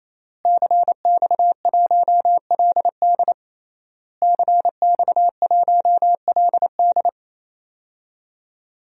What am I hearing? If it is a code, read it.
Morse code, heard as a keyed tone, about 28 wpm, 700 Hz. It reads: CX1LD CX1LD
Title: DXCC-100 Call Signs - 5 Characters - Repeated 3x - 28wpm